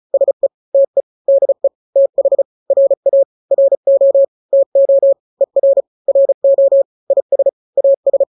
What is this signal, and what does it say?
Morse code, heard as a keyed tone, about 35 wpm, 550 Hz. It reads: SE TE DE TH RA RO TO ER RO IS AS
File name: n-grams / 2-grams 35 30 WPM